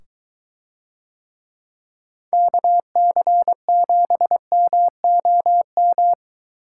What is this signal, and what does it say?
Morse code, heard as a keyed tone, about 23 wpm, 700 Hz. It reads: KC7MOM